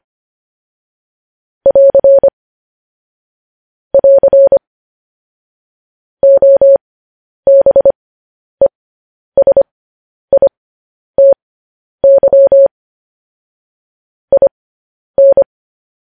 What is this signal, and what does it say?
Morse code, heard as a keyed tone, about 25 wpm, spaced as for 10 wpm, 550 Hz. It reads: <AR> <AR> OBESITY IN
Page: News Headlines in Morse Code at 10 WPM